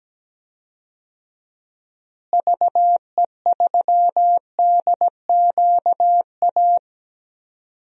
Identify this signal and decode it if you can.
Morse code, heard as a keyed tone, about 17 wpm, 700 Hz. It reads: VE3DQA
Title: DXCC-100 Call Signs - 6 Characters - Repeated 3x - 17wpm